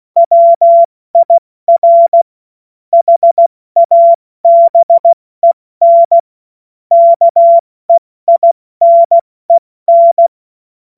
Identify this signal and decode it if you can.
Morse code, heard as a keyed tone, about 16 wpm, 680 Hz. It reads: WIR HABEN KEINEN
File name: Morsekurs / Witze / Witz 25. Mai 2021, 70 F80